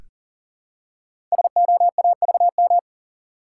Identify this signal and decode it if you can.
Morse code, heard as a keyed tone, about 40 wpm, 700 Hz. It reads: SOAVM
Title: Code Groups - 5 Letters - 40wpm